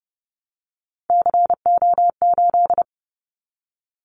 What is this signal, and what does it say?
Morse code, heard as a keyed tone, about 29 wpm, 700 Hz. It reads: CO8
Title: Callsign Prefixes for Top 100 DXCC - 30wpm